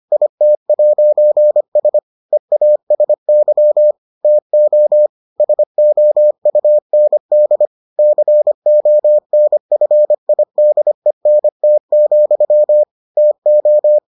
Morse code, heard as a keyed tone, about 25 wpm, 600 Hz. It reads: IT'S EASY TO SOUND CONFIDENT, TO